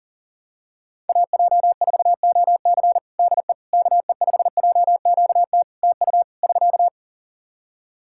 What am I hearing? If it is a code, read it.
Morse code, heard as a keyed tone, about 40 wpm, 700 Hz. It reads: AJ4OC DE KE5JQT TU <SK>
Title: K1USN SST QSO Practice 40wpm